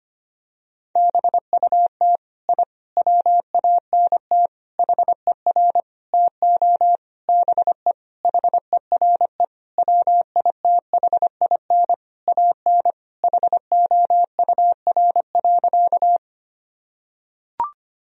Morse code, heard as a keyed tone, about 25 wpm, 700 Hz. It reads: BUT I WANT HER TO BE HERE WITHIN AN HOUR.
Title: Sentences from Top 1000 Words - Review Part 4 - 25wpm